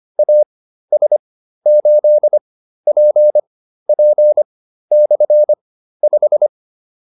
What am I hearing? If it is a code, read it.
Morse code, heard as a keyed tone, about 25 wpm, spaced as for 13 wpm, 600 Hz. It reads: AS8PP/5